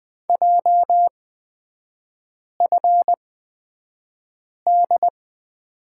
Morse code, heard as a keyed tone, about 20 wpm, spaced as for 5 wpm, 700 Hz. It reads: JFD
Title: 250 Lesson 40e - BK - Three Characters 20wpm